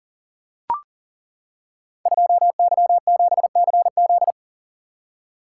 Morse code, heard as a keyed tone, about 39 wpm, 690 Hz. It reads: 2Y7CZ
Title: Code Groups - 5 Letter:Numbers - 40wpm